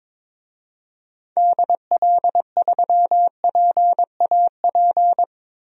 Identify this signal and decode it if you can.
Morse code, heard as a keyed tone, about 22 wpm, 700 Hz. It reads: DL3PAP
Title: DXCC-100 Call Signs - 6 Characters - Repeated 2x - 22wpm